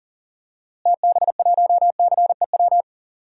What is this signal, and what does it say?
Morse code, heard as a keyed tone, about 40 wpm, 700 Hz. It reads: TB1CEW
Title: DXCC-100 Call Signs - 6 Characters - Repeated 2x - 40wpm